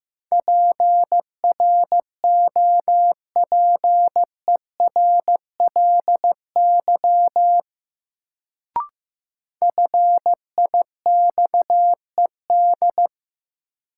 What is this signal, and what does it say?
Morse code, heard as a keyed tone, about 15 wpm, 700 Hz. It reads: PROPERLY FIXED
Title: Sentences from Top 1100 Words - New Word Drill - 15wpm